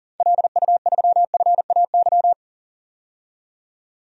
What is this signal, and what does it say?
Morse code, heard as a keyed tone, about 40 wpm, 700 Hz. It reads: LU3FAY
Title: DXCC-100 Call Signs - 6 Characters - 40wpm